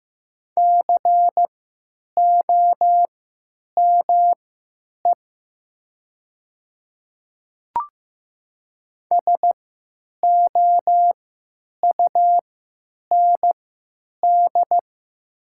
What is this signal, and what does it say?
Morse code, heard as a keyed tone, about 15 wpm, spaced as for 9 wpm, 700 Hz. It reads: COME SOUND
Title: 3x Character Spacing - Top 100 Words - 15wpm